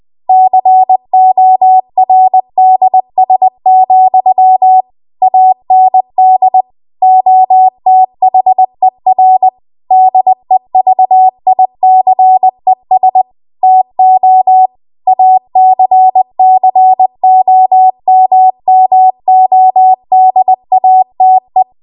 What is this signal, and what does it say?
Morse code, heard as a keyed tone, about 20 wpm, 750 Hz. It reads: CORDS, AND OTHER DEVICES TO ACCOMMODATE